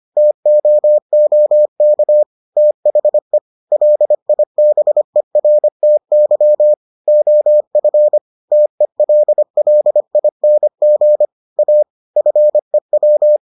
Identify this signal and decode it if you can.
Morse code, heard as a keyed tone, about 25 wpm, 600 Hz. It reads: TOOK THE LIBERTY OF TELLING A FEW